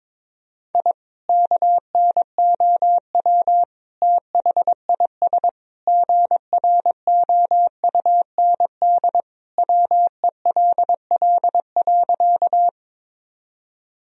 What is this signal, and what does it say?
Morse code, heard as a keyed tone, about 22 wpm, 700 Hz. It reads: I KNOW THIS GROUND WELL.
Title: Sentences from Top 1000 Words - Review Part 2 - 22wpm